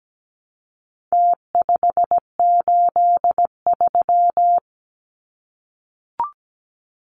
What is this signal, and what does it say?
Morse code, heard as a keyed tone, about 17 wpm, 700 Hz. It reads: T583
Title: Numbers - Contest Style - 4-Digit 17wpm